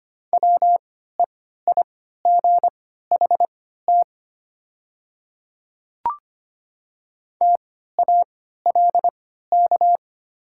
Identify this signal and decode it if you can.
Morse code, heard as a keyed tone, about 25 wpm, spaced as for 14 wpm, 700 Hz. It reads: WEIGHT TALK
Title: 3x Character Spacing - Top 500 Words - 25wpm